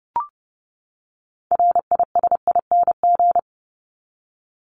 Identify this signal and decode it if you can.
Morse code, heard as a keyed tone, about 30 wpm, 700 Hz. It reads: RISING
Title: ING Suffix - Top 100 Words with Base Words 30wpm